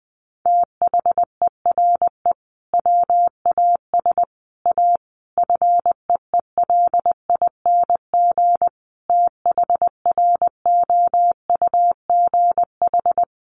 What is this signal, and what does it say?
Morse code, heard as a keyed tone, about 20 wpm, 700 Hz. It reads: THERE WAS A FEELING THROUGH
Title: Sentences from Top 1200 Words 20wpm